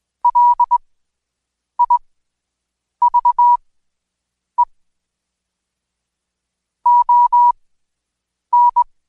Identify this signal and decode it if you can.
Morse code, heard as a keyed tone, about 21 wpm, spaced as for 7 wpm, 970 Hz. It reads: LIVE ON
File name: Code Quick / AudioDownld2 / Seg113-Day18-DayDrill 18